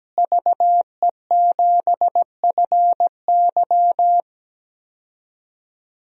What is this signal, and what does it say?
Morse code, heard as a keyed tone, about 17 wpm, 700 Hz. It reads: VE7FY
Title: DXCC-100 Call Signs - 5 Characters - Repeated 3x - 17wpm